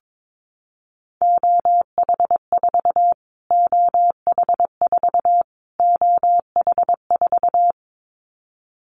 Morse code, heard as a keyed tone, about 22 wpm, 700 Hz. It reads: OH4 OH4 OH4
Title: Callsign Prefixes for Top 25 DXCC - Repeated 3x - 22wpm